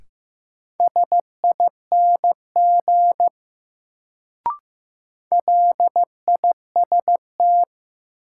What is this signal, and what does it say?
Morse code, heard as a keyed tone, about 15 wpm, 700 Hz. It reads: SING LIST
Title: Sentences from Top 400 Words - New Word Drill - 15wpm